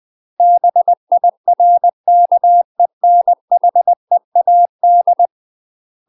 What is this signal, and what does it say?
Morse code, heard as a keyed tone, about 20 wpm, 700 Hz. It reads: BIRKENHEAD
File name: UK MASTER QSOs TRAINING / 20WPM / UK MASTER QSOs 0030